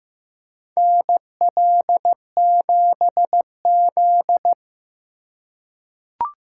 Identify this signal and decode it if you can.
Morse code, heard as a keyed tone, about 16 wpm, 700 Hz. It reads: NL7Z
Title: USA Call Signs - 2x Character Spacing - 15wpm